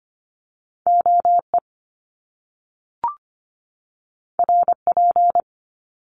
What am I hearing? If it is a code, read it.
Morse code, heard as a keyed tone, about 25 wpm, 700 Hz. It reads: OE RP
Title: Top Two Letter Combinations in Words 25wpm